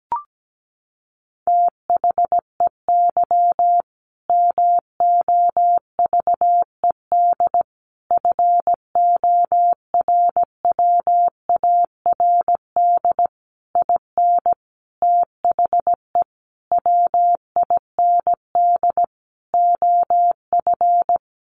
Morse code, heard as a keyed tone, about 17 wpm, 700 Hz. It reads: THEY MOVED FORWARD IN THE WIND OF
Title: A Sound Of Thunder 17wpm